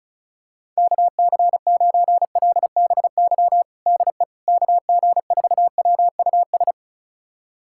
Morse code, heard as a keyed tone, about 36 wpm, 700 Hz. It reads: KC9LBY DE KG4WUS
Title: USA Callsign DE Callsign - 6 Characters - 35wpm